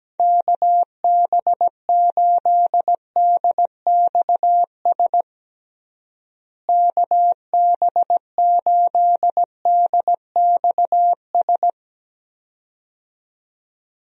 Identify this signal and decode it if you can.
Morse code, heard as a keyed tone, about 17 wpm, 700 Hz. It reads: KB8DXS KB8DXS
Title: USA Call Signs - 6 Characters - Repeated 2x - 17wpm